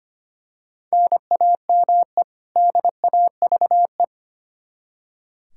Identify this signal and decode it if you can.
Morse code, heard as a keyed tone, about 25 wpm, 700 Hz. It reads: NAME DAVE
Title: Practice QSO - Exchange 1 - Beginning - 25wpm